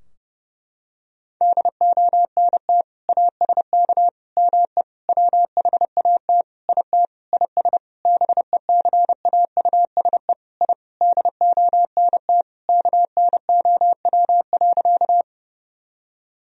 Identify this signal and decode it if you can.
Morse code, heard as a keyed tone, about 32 wpm, 700 Hz. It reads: DONT ASK ME WHAT IT IS BECAUSE I DONT KNOW.
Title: Sentences from Top 1000 Words - Review Part 3 - 30wpm